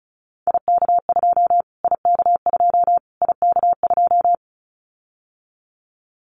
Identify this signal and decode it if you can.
Morse code, heard as a keyed tone, about 34 wpm, 700 Hz. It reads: IK2 IK2 IK2
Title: Callsign Prefixes for Top 25 DXCC - Repeated 3x - 35wpm